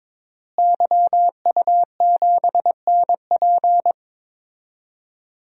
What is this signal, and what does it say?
Morse code, heard as a keyed tone, about 21 wpm, 700 Hz. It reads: YU7NP
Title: DXCC-100 Call Signs - 5 Characters - Repeated 2x - 22wpm